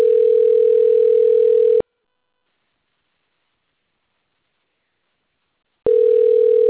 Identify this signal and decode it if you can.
Ringback tone (North America), 440 + 480 Hz.